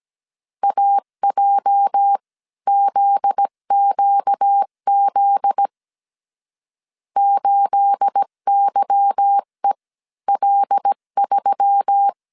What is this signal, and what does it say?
Morse code, heard as a keyed tone, about 17 wpm, spaced as for 14 wpm, 800 Hz. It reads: AJZQZ 8YEL3
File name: 1500C S B 16wpm